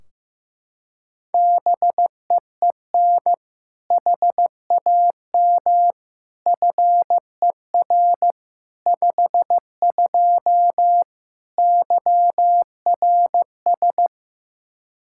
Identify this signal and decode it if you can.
Morse code, heard as a keyed tone, about 15 wpm, 700 Hz. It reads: BEEN HAM FER 52 YRS
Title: Practice QSO - Exchange 3 - More Info - 15wpm